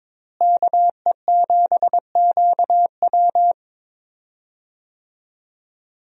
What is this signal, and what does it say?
Morse code, heard as a keyed tone, about 22 wpm, 700 Hz. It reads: KE7QW
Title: USA Call Signs - 5 Characters - Repeated 2x - 22wpm